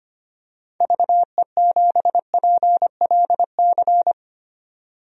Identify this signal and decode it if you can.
Morse code, heard as a keyed tone, about 25 wpm, 700 Hz. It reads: VE7PLC
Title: DXCC-100 Call Signs - 6 Characters - 25wpm